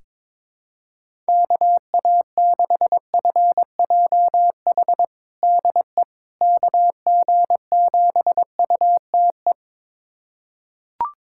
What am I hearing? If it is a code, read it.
Morse code, heard as a keyed tone, about 22 wpm, 700 Hz. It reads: KA6FJH DE KG7UTE
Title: USA Callsign DE Callsign - 6 Characters - 22wpm